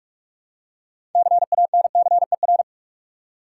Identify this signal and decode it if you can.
Morse code, heard as a keyed tone, about 46 wpm, 700 Hz. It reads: CANCER
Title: Six Letter Words - 45wpm